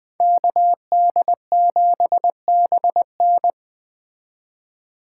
Morse code, heard as a keyed tone, about 20 wpm, 700 Hz. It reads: KD7BN
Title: Speed Racing - USA 2x2 Call Signs 17wpm